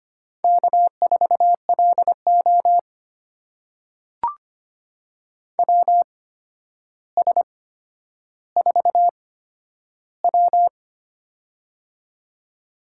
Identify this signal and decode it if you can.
Morse code, heard as a keyed tone, about 25 wpm, spaced as for 11 wpm, 700 Hz. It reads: K4LO WS4W
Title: USA Call Signs - 4 Characters - 8x Character Spacing - 25wpm